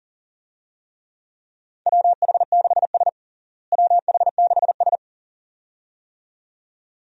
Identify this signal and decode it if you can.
Morse code, heard as a keyed tone, about 40 wpm, 700 Hz. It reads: WH6S WH6S
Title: USA Call Signs - 4 Characters - Repeated 2x - 40wpm